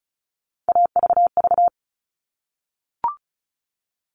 Morse code, heard as a keyed tone, about 34 wpm, 700 Hz. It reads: AVV
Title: All Cut Numbers - 3-Digit 35wpm